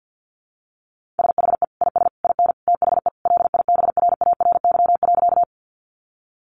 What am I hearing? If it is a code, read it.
Morse code, heard as a keyed tone, about 50 wpm, 700 Hz. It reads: SHE IS IN THE LIBRARY.
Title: Sentences from Top 600 Words 50wpm